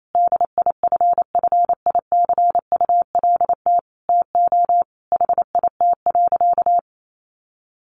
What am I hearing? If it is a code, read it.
Morse code, heard as a keyed tone, about 28 wpm, 700 Hz. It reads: DIFFICULT TO HIT.
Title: Sentences from Top 1200 Words 28wpm